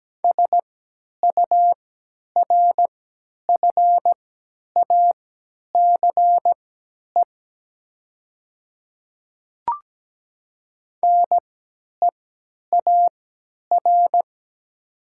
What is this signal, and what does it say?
Morse code, heard as a keyed tone, about 17 wpm, spaced as for 10 wpm, 700 Hz. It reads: SURFACE NEAR
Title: 3x Character Spacing - Top 500 Words - 17wpm